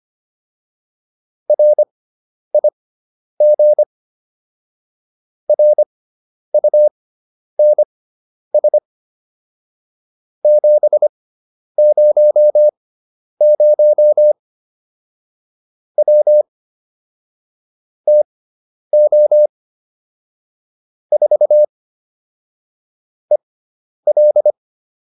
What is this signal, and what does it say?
Morse code, heard as a keyed tone, about 25 wpm, spaced as for 10 wpm, 600 Hz. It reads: RIG RUNS 700 W TO 4 EL